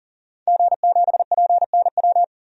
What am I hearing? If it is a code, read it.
Morse code, heard as a keyed tone, about 40 wpm, 700 Hz. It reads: G7PNW
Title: DXCC-100 Call Signs - 5 Characters - 40wpm